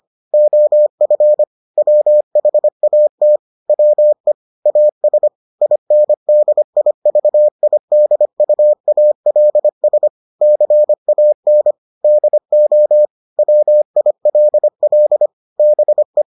Morse code, heard as a keyed tone, about 25 wpm, 600 Hz. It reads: OF WHAT WE AS INDIVIDUALS CAN DO WILL BE